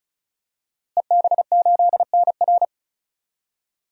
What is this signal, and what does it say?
Morse code, heard as a keyed tone, about 36 wpm, 700 Hz. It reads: EB8NR
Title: DXCC-100 Call Signs - 5 Characters - Repeated 2x - 35wpm